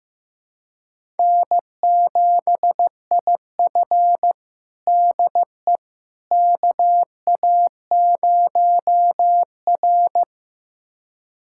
Morse code, heard as a keyed tone, about 16 wpm, 700 Hz. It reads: N7IF DE KA0R
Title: USA Callsign DE Callsign - 4 Characters - 15wpm